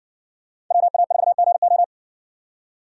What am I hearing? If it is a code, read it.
Morse code, heard as a keyed tone, about 58 wpm, 700 Hz. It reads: WA4GO